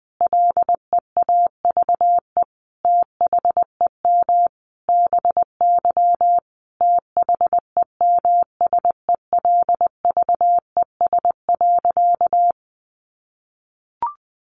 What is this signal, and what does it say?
Morse code, heard as a keyed tone, about 20 wpm, 700 Hz. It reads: LEAVE THEM BY THEMSELVES.
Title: Sentences from Top 800 Words 20wpm